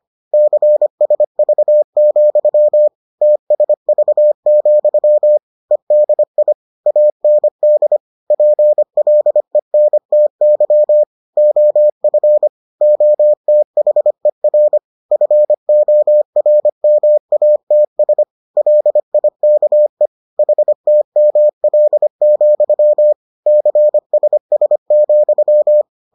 Morse code, heard as a keyed tone, about 25 wpm, 600 Hz. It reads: CSV, TSV, EDI AND PLENTY OF OTHER FORMATS LIKE HTML, CSS,